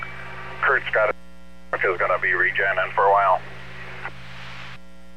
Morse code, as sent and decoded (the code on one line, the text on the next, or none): none